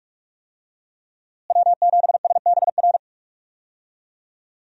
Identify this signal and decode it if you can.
Morse code, heard as a keyed tone, about 41 wpm, 700 Hz. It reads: W7SBR
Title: USA Call Signs - 5 Characters - Repeated 3x - 45wpm